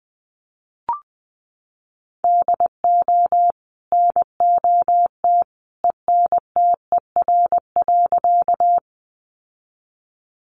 Morse code, heard as a keyed tone, about 20 wpm, 700 Hz. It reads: DO NOT ENTER.
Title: Sentences from Top 1300 Words 20wpm